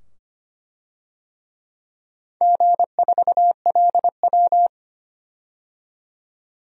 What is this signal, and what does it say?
Morse code, heard as a keyed tone, about 25 wpm, 700 Hz. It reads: G4LW
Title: DXCC-100 Call Signs - 4 Characters - 25wpm